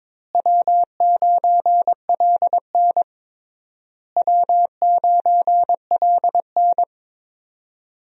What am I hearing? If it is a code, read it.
Morse code, heard as a keyed tone, about 22 wpm, 700 Hz. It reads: W9LN W9LN
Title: USA Call Signs - 4 Characters - Repeated 3x - 22wpm